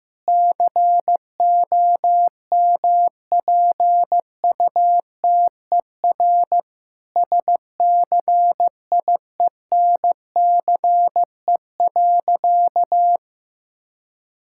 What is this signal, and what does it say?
Morse code, heard as a keyed tone, about 16 wpm, 700 Hz. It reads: COMPUTER SCIENCE.
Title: Sentences from Top 1000 Words - Review Part 6 - 15wpm